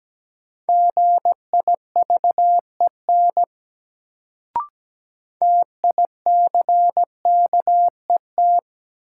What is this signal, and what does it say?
Morse code, heard as a keyed tone, about 17 wpm, 700 Hz. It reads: GIVEN TICKET
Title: Sentences from Top 1400 Words - New Word Drill Alt - 17wpm